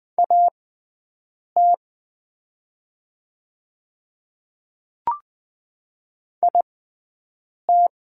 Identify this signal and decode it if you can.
Morse code, heard as a keyed tone, about 20 wpm, spaced as for 7 wpm, 700 Hz. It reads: AT IT
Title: 6x Character Spacing - Top 100 Words - 20wpm